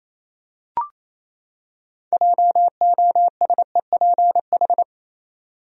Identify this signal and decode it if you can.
Morse code, heard as a keyed tone, about 28 wpm, 700 Hz. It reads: JOSEPH
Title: USA Common Names - 28wpm